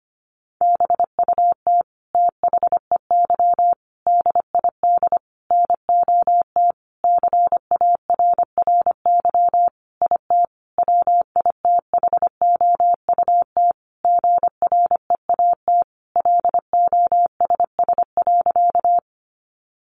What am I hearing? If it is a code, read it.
Morse code, heard as a keyed tone, about 25 wpm, 700 Hz. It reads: BUT THEY DID NOT CARRY IT WITHOUT GREAT LOSS.
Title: Sentences from Top 1400 Words 25wpm